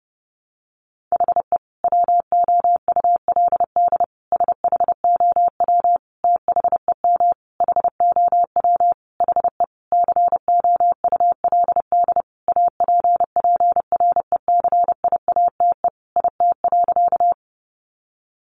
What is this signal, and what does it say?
Morse code, heard as a keyed tone, about 28 wpm, 700 Hz. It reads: HE WOULD SHOW THEM HOW HE COULD APPRECIATE IT.
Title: Sentences from Top 1300 Words 30wpm